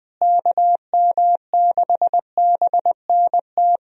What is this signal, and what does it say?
Morse code, heard as a keyed tone, about 20 wpm, 700 Hz. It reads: KM6BNT
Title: USA Call Signs - 6 Characters - Repeated 2x - 20wpm